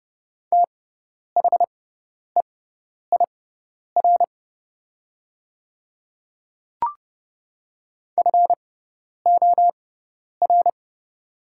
Morse code, heard as a keyed tone, about 30 wpm, spaced as for 10 wpm, 700 Hz. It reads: THEIR FOR
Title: 6x Character Spacing - Top 100 Words - 30wpm